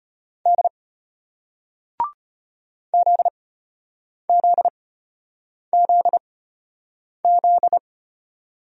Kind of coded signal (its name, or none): Morse code